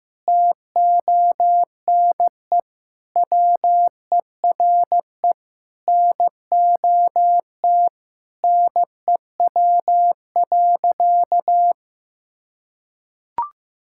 Morse code, heard as a keyed tone, about 16 wpm, 700 Hz. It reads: TONE WERE NOT NEW.
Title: Sentences from Top 1600 Words 15wpm